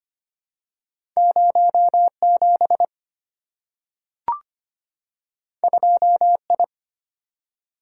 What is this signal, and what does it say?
Morse code, heard as a keyed tone, about 25 wpm, 700 Hz. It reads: 07 2I
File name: Two Letter-Number Combinations - 25wpm